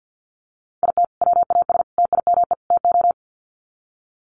Morse code, heard as a keyed tone, about 50 wpm, 700 Hz. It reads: IT WAS TIME TO
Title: Sets of 4 Words 50wpm